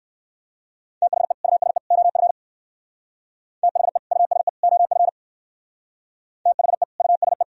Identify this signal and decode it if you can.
Morse code, heard as a keyed tone, about 55 wpm, 700 Hz. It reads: THE USE OF THE USE OF THE USE
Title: Speed-Racing - Sets of 3 Words 45wpm